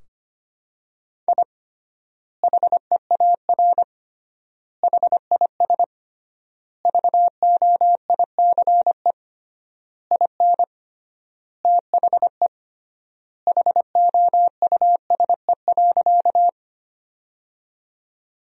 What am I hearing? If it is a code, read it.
Morse code, heard as a keyed tone, about 25 wpm, 700 Hz. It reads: I HEAR HIS VOICE IN THE HOUSE.
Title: 3x Word-Spacing - Sentences from Top 500 Words - 25wpm